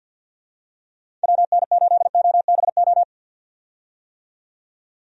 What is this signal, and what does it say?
Morse code, heard as a keyed tone, about 50 wpm, 700 Hz. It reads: WN8OBO
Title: USA Call Signs - 6 Characters - 50wpm